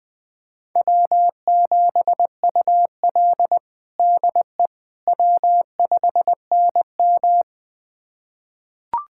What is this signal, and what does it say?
Morse code, heard as a keyed tone, about 20 wpm, 700 Hz. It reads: W7UL DE W5NM
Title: USA Callsign DE Callsign - 4 Characters - 20wpm